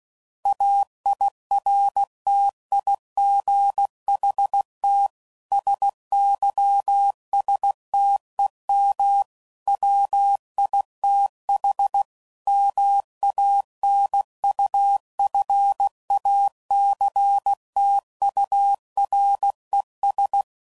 Morse code, heard as a keyed tone, about 16 wpm, 800 Hz. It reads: AIRTIGHT SYSTEM WITH MANUFACTURES